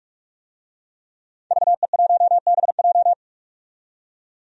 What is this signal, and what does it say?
Morse code, heard as a keyed tone, about 42 wpm, 700 Hz. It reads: VE1BJ
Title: DXCC-100 Call Signs - 5 Characters - 45wpm